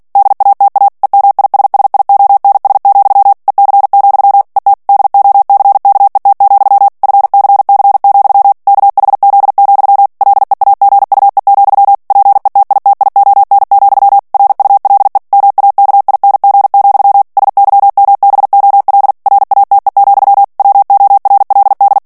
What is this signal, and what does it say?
Morse code, heard as a keyed tone, about 48 wpm, 800 Hz. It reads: DATA EMISSIONS, EC, ET DOCKET, FCC, KHZ, LEAGUE, PETITION, RULE MAKING, SYMBOL RATE, WORLD
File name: ANGLAIS texte clair 3 vitesse 45 mots minute